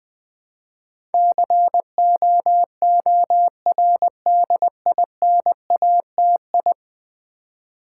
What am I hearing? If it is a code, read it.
Morse code, heard as a keyed tone, about 20 wpm, 700 Hz. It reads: COORDINATI
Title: Build A Word - 5 Letters At A Time - 12-Letter Words - 20wpm